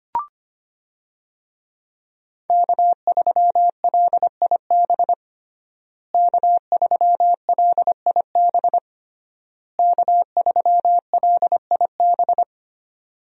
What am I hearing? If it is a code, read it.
Morse code, heard as a keyed tone, about 25 wpm, 700 Hz. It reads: K3LIB K3LIB K3LIB
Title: USA Call Signs - 5 Characters - Repeated 3x - 25wpm